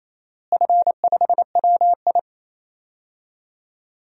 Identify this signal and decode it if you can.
Morse code, heard as a keyed tone, about 28 wpm, 700 Hz. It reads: F5WI
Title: DXCC-100 Call Signs - 4 Characters - Repeated 3x - 28wpm